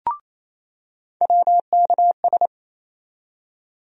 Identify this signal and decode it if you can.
Morse code, heard as a keyed tone, about 28 wpm, 700 Hz. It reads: WKS
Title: USA Call Sign Postfixes - 3 Letters 28wpm